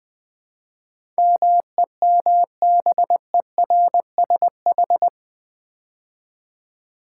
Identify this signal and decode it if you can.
Morse code, heard as a keyed tone, about 20 wpm, 700 Hz. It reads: MEMBERSH
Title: Build A Word - 4 Letters At A Time - 10-Letter Words - 20wpm